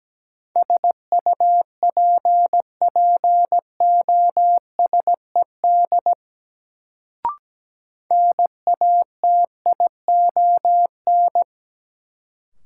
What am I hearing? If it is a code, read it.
Morse code, heard as a keyed tone, about 17 wpm, 700 Hz. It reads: SUPPOSED NATION
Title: Sets of 3 Words - New Word Drill - 17wpm